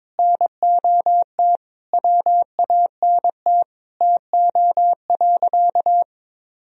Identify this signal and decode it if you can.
Morse code, heard as a keyed tone, about 22 wpm, 700 Hz. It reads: NOT WANT TO.
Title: Sentences from Top 1000 Words - Review Part 3 - 22wpm